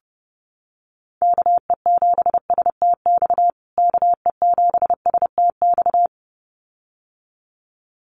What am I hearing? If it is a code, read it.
Morse code, heard as a keyed tone, about 28 wpm, 700 Hz. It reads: KE7STX KE7STX
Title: USA Call Signs Repeated 2x 30wpm